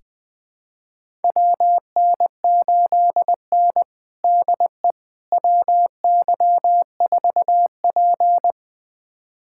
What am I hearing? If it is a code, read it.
Morse code, heard as a keyed tone, about 21 wpm, 700 Hz. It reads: WN8N DE WY4P
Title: USA Callsign DE Callsign - 4 Characters - 20wpm